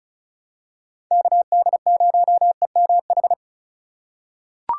Morse code, heard as a keyed tone, about 36 wpm, 700 Hz. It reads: KD0EMH
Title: USA Call Signs - 6 Characters - 3x Character Spacing - 35wpm